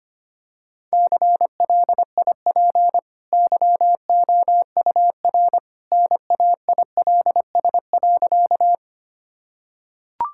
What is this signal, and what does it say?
Morse code, heard as a keyed tone, about 25 wpm, 700 Hz. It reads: CLIP YOUR NAILS.
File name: Easy Sentences with Hints - 25wpm